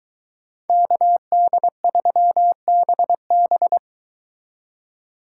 Morse code, heard as a keyed tone, about 23 wpm, 700 Hz. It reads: KD3BB